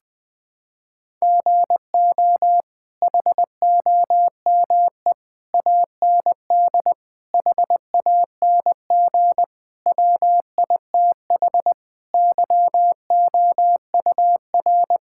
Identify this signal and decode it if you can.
Morse code, heard as a keyed tone, about 20 wpm, 700 Hz. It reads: GO HOME AND HANG WITH YOUR
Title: Sentences from Top 1600 Words 20wpm